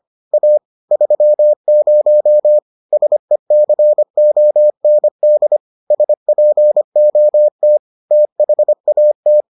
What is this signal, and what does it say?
Morse code, heard as a keyed tone, about 25 wpm, 600 Hz. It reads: A 30 SECOND SPOT THAT